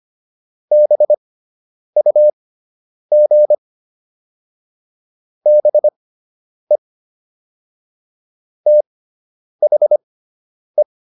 Morse code, heard as a keyed tone, about 25 wpm, spaced as for 9 wpm, 600 Hz. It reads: BUG BE THE